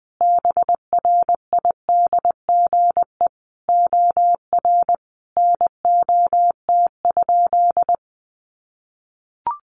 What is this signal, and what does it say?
Morse code, heard as a keyed tone, about 19 wpm, 700 Hz. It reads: BRIDGE OR NOT?
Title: Sentences from Top 1300 Words 20wpm